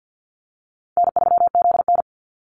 Morse code, heard as a keyed tone, about 50 wpm, 700 Hz. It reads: N3ZN
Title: Easy Call Signs Repeated 2x - 50wpm